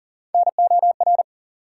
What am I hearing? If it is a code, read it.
Morse code, heard as a keyed tone, about 40 wpm, 700 Hz. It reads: NOR
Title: Sentences from Top 1500 Words - New Word Drill Alt - 40wpm